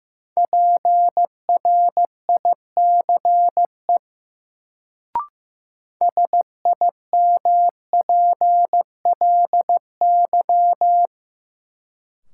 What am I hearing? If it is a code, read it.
Morse code, heard as a keyed tone, about 15 wpm, 700 Hz. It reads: PRICE SIMPLY
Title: Sentences from Top 600 Words - New Word Drill - 15wpm